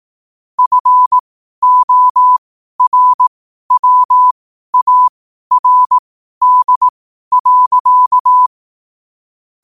Morse code, heard as a keyed tone, about 18 wpm, spaced as for 12 wpm, 1000 Hz. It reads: FORWARD.
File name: qotdfiles / 12wpm